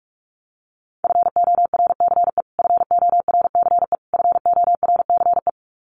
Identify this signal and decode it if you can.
Morse code, heard as a keyed tone, about 44 wpm, 700 Hz. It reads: FORCE FORCE FORCE
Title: Top 500 Words Repeated 45wpm